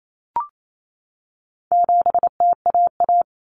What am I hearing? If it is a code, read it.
Morse code, heard as a keyed tone, about 28 wpm, 700 Hz. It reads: 7TAA
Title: Common Cut Numbers - 4-Digit 28wpm